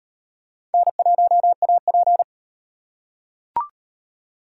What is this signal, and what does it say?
Morse code, heard as a keyed tone, about 35 wpm, 700 Hz. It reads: N1AP